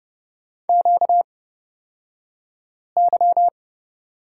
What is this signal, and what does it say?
Morse code, heard as a keyed tone, about 31 wpm, spaced as for 5 wpm, 700 Hz. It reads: QY
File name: Lesson 38d - Letter q - Two Characters 30wpm